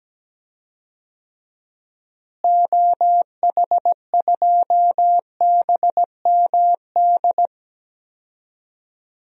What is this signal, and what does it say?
Morse code, heard as a keyed tone, about 17 wpm, 700 Hz. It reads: OH2BMD
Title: DXCC-100 Call Signs - 6 Characters - Repeated 3x - 17wpm